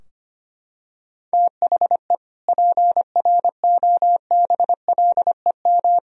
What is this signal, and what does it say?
Morse code, heard as a keyed tone, about 25 wpm, 700 Hz. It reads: THE PROBLEM
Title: Sentences from Top 1000 Words - Review Part 2 - 25wpm